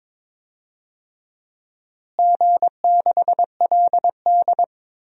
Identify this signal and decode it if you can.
Morse code, heard as a keyed tone, about 22 wpm, 700 Hz. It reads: G6LD
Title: DXCC-100 Call Signs - 4 Characters - Repeated 2x - 22wpm